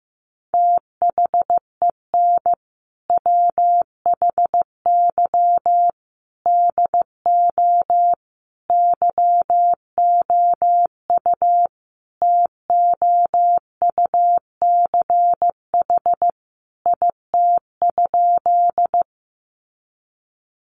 Morse code, heard as a keyed tone, about 15 wpm, 700 Hz. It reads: THEN WHY DO YOU TOUCH IT?
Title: Sentences from Top 1200 Words 15wpm